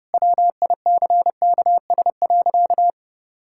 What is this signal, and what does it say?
Morse code, heard as a keyed tone, about 30 wpm, 700 Hz. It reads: WICKS.
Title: Pangram Challenge v2 - 30wpm